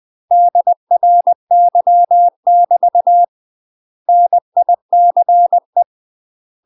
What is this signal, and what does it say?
Morse code, heard as a keyed tone, about 20 wpm, 700 Hz. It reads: DRY <BT> NICE
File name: UK MASTER QSOs TRAINING / 20WPM / UK MASTER QSOs 0058